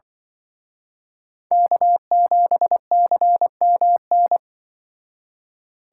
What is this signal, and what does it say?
Morse code, heard as a keyed tone, about 24 wpm, 700 Hz. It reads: K7CMN
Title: Speed Racing - USA 1x3 Call Signs 20wpm